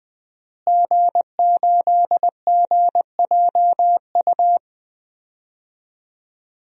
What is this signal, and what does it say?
Morse code, heard as a keyed tone, about 20 wpm, 700 Hz. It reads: G8GJU
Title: DXCC-100 Call Signs - 5 Characters - Repeated 2x - 20wpm